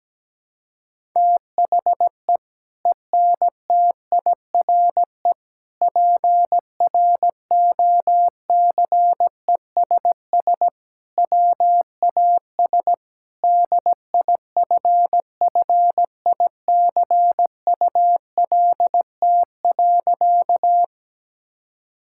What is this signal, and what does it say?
Morse code, heard as a keyed tone, about 17 wpm, 700 Hz. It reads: THE ENTIRE PROCESS WAS DIFFICULT.
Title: Sentences from Top 1000 Words - Review Part 5 - 17wpm